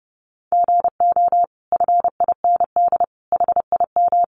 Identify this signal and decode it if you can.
Morse code, heard as a keyed tone, about 30 wpm, 700 Hz. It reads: GO FIND HIM
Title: Sentences from Top 1100 Words 30wpm